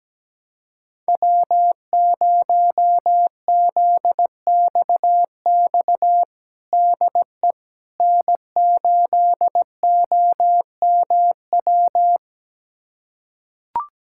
Morse code, heard as a keyed tone, about 17 wpm, 700 Hz. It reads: W0ZXX DE N8OMW
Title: USA Callsign DE Callsign - 5 Characters - 17wpm